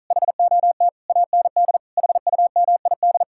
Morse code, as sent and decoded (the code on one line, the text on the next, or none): .... --- - / .- -. -.. / .... ..- -- .. -..
HOT AND HUMID